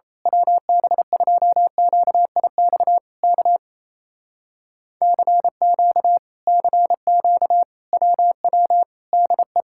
Morse code, heard as a keyed tone, about 26 wpm, 700 Hz. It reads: WB2QIX K CQ CQ WW DE
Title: Speed-Racing Common QSO Phrases - 28wpm